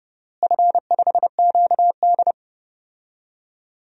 Morse code, heard as a keyed tone, about 31 wpm, 700 Hz. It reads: F5QD
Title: DXCC-100 Call Signs - 4 Characters - Repeated 2x - 30wpm